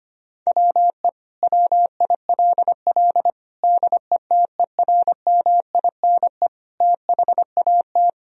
Morse code, heard as a keyed tone, about 25 wpm, 700 Hz. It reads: WE WILL DETERMINE THAT
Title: Sentences from Top 1000 Words - Review Part 4 - 25wpm